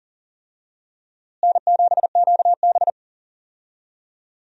Morse code, heard as a keyed tone, about 37 wpm, 700 Hz. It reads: N7QB
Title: USA Call Signs - 4 Characters - Repeated 3x - 40wpm